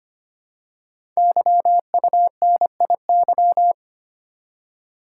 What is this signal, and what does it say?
Morse code, heard as a keyed tone, about 25 wpm, 700 Hz. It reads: YUNIY
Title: Code Groups - 5 Letters - 25wpm